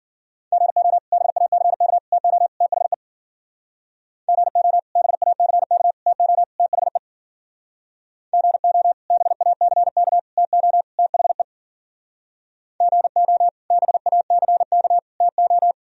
Morse code, heard as a keyed tone, about 53 wpm, 700 Hz. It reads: GO BACK TO THE GO BACK TO THE GO BACK TO THE GO BACK TO